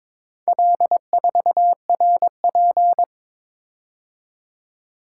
Morse code, heard as a keyed tone, about 22 wpm, 700 Hz. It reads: L4RP
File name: practice / 22wpm / Four Letter-Number Combinations - 22wpm